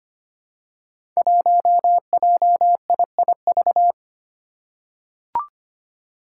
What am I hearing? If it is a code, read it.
Morse code, heard as a keyed tone, about 25 wpm, 700 Hz. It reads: 1JIIV
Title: Five Challenging Letter-Number Combinations 25wpm